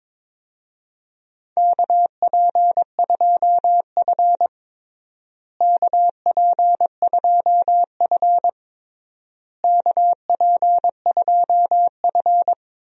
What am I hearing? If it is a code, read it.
Morse code, heard as a keyed tone, about 22 wpm, 700 Hz. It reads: KP2F KP2F KP2F
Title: DXCC-100 Call Signs - 4 Characters - Repeated 3x - 22wpm